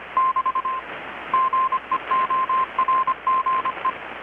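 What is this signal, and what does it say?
Morse code, heard as a keyed tone, about 25 wpm, 1050 Hz, with radio noise. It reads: <BT> GEORGE
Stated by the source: Morse from KPH's Night of Nights XX maritime radiotelegraphy event, received in Chiapas, Mexico about 3,800 km away on a Kenwood R-600 with a grounded 10 m long-wire antenna.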